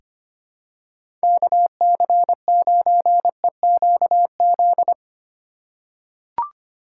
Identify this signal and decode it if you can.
Morse code, heard as a keyed tone, about 25 wpm, 700 Hz. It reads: KC9EQZ
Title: USA Call Signs - 6 Characters - 2x Character Spacing - 25wpm